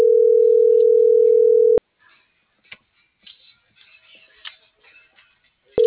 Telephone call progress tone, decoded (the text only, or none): ringback tone